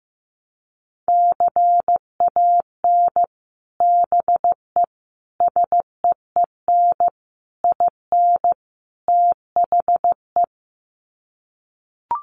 Morse code, heard as a keyed tone, about 14 wpm, 700 Hz. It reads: CAN BE SEEN IN THE
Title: Sets of 5 Words Encore2 15wpm